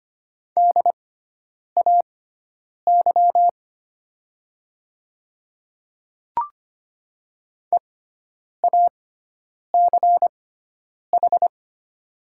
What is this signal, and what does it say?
Morse code, heard as a keyed tone, about 25 wpm, spaced as for 9 wpm, 700 Hz. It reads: DAY EACH
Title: 6x Character Spacing - Top 100 Words - 25wpm